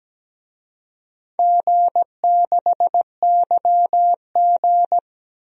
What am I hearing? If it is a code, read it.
Morse code, heard as a keyed tone, about 17 wpm, 700 Hz. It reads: G6YG
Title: DXCC-100 Call Signs - 4 Characters - Repeated 2x - 17wpm